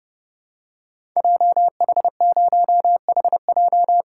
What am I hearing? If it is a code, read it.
Morse code, heard as a keyed tone, about 30 wpm, 700 Hz. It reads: JH0HJ
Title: Code Groups - 5 Letter:Numbers - 30wpm